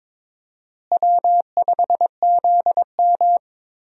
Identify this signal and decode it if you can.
Morse code, heard as a keyed tone, about 22 wpm, 700 Hz. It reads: W5ZM
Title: USA Call Signs - 4 Characters - Repeated 2x - 22wpm